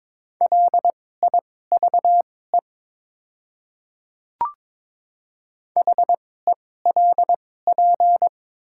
Morse code, heard as a keyed tone, about 22 wpm, spaced as for 16 wpm, 700 Hz. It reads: LIVE HELP
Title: 2x Character Spacing - Top 200 Words - 22wpm